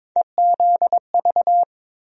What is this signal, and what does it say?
Morse code, heard as a keyed tone, about 22 wpm, 700 Hz. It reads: EZV
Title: USA Call Sign Postfixes - 3 Letters 22wpm